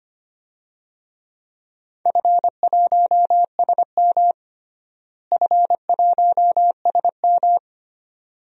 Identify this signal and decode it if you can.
Morse code, heard as a keyed tone, about 25 wpm, 700 Hz. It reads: F1SM F1SM
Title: DXCC-100 Call Signs - 4 Characters - Repeated 2x - 25wpm